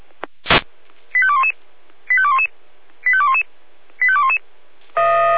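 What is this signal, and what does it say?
Five-tone selective call (ZVEI), heard as quick runs of five tones, tones of about 70 ms. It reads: 85211 85211 85211 85211
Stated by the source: ZVEI five-tone selective call 85830 received on an SDR.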